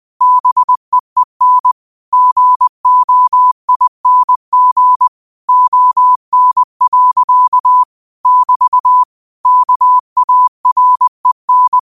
Morse code, heard as a keyed tone, about 20 wpm, 1000 Hz. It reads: BEEN GOING ON. <BT> KAREN